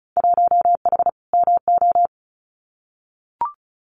Morse code, heard as a keyed tone, about 34 wpm, 700 Hz. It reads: 1H MO
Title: Winter Field Day Exchanges 35wpm